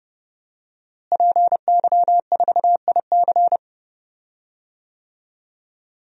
Morse code, heard as a keyed tone, about 29 wpm, 700 Hz. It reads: PY4IC
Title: DXCC-100 Call Signs - 5 Characters - Repeated 2x - 30wpm